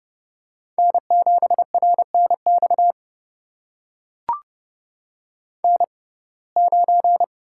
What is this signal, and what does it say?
Morse code, heard as a keyed tone, about 32 wpm, spaced as for 22 wpm, 700 Hz. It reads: N7RNX N9
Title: USA Call Signs - 5 Characters - 6x Character Spacing - 30wpm